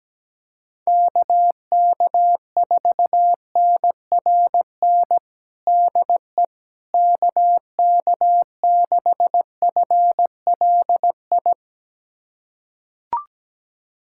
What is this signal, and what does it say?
Morse code, heard as a keyed tone, about 17 wpm, 700 Hz. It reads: KK4NRN DE KK6FLI
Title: USA Callsign DE Callsign - 6 Characters - 17wpm